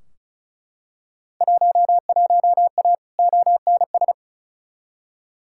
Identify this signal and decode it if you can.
Morse code, heard as a keyed tone, about 34 wpm, 700 Hz. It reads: 11A ONS
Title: Field Day Exchanges Encore - 35wpm